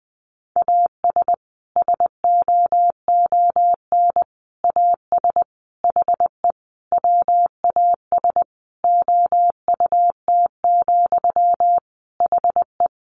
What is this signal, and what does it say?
Morse code, heard as a keyed tone, about 20 wpm, 700 Hz. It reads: AS SOON AS HE WAS OUT, HE
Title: Sentences from Top 1500 Words 20wpm